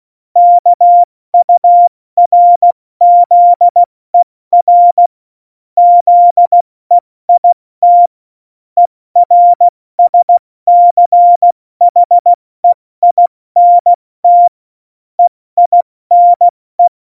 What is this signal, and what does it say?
Morse code, heard as a keyed tone, about 16 wpm, 700 Hz. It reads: KURZER ZEIT ERSCHEINT EINE
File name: Morsekurs / Witze / Witz 22. Januar 2019, 70 F80